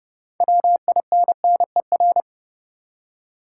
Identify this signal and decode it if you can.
Morse code, heard as a keyed tone, about 30 wpm, 700 Hz. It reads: WINNER
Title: Six Letter Words - 30wpm